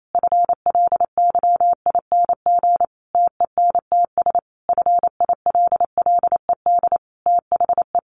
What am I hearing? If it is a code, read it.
Morse code, heard as a keyed tone, about 28 wpm, 700 Hz. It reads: FLYING TENTS FILLED THE
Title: A Sound Of Thunder 28wpm